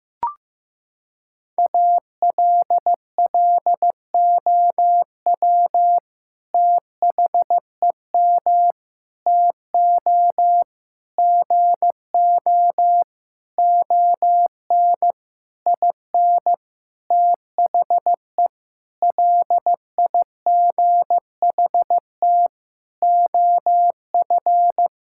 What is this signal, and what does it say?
Morse code, heard as a keyed tone, about 15 wpm, 700 Hz. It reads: ALLOW THEM TO GO ON IN THE LIGHT OF
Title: Sentences from Top 1000 Words - Review Part 5 - 15wpm